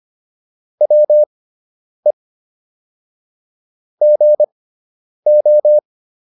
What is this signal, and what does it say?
Morse code, heard as a keyed tone, about 25 wpm, spaced as for 9 wpm, 600 Hz. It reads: WE GO